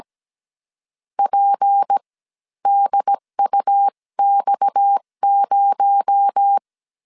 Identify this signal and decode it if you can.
Morse code, heard as a keyed tone, about 17 wpm, spaced as for 13 wpm, 800 Hz. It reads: PDUX0
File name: CW / DET / 1 6 1500C S C 16wpm 2733